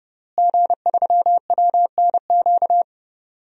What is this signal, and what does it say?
Morse code, heard as a keyed tone, about 31 wpm, 700 Hz. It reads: G3WNQ
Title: DXCC-100 Call Signs - 5 Characters - Repeated 3x - 30wpm